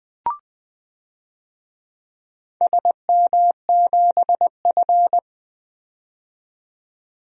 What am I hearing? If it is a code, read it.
Morse code, heard as a keyed tone, about 20 wpm, 700 Hz. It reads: SM7F
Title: DXCC-100 Call Signs - 4 Characters - 20wpm